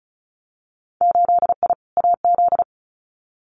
Morse code, heard as a keyed tone, about 36 wpm, 700 Hz. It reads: 8I AZ
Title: Winter Field Day Exchanges Encore 35wpm